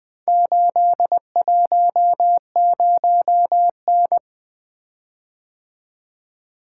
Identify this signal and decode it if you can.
Morse code, heard as a keyed tone, about 20 wpm, 700 Hz. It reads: 810N
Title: Four Letter-Number Combinations - 20wpm